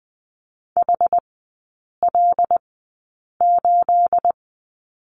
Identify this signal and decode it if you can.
Morse code, heard as a keyed tone, about 20 wpm, spaced as for 8 wpm, 690 Hz. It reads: HL8
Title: 212 Lesson 34e - Number 8 - Three Characters 20wpm